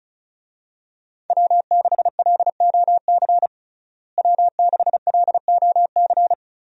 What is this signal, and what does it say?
Morse code, heard as a keyed tone, about 34 wpm, 700 Hz. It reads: W6LOC W6LOC
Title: USA Call Signs - 5 Characters - Repeated 2x - 35wpm